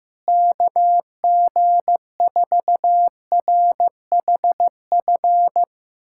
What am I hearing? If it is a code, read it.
Morse code, heard as a keyed tone, about 16 wpm, 700 Hz. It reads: KG4RHF
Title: USA Callsign DE Callsign - 6 Characters - 15wpm